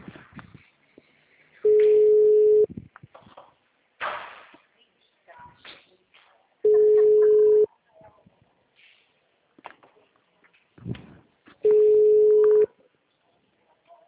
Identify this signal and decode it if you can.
Ringback tone (Europe, 425 Hz), with speech.
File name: out- -2003- - - .280